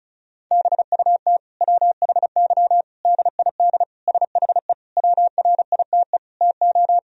Morse code, heard as a keyed tone, about 34 wpm, 700 Hz. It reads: BUT WHY DID SHE WRITE TO
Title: Sentences from Top 1000 Words - Review Part 1 - 35wpm